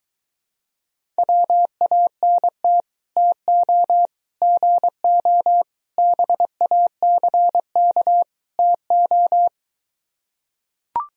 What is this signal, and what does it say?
Morse code, heard as a keyed tone, about 23 wpm, 700 Hz. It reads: WANT TO GO BACK TO